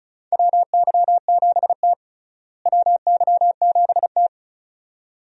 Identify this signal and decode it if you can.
Morse code, heard as a keyed tone, about 36 wpm, 700 Hz. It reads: WY7T WY7T
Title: USA Call Signs - 4 Characters - Repeated 2x - 35wpm